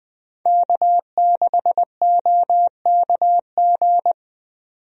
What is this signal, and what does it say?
Morse code, heard as a keyed tone, about 20 wpm, 700 Hz. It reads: K6OKG
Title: USA Call Signs - 5 Characters - 20wpm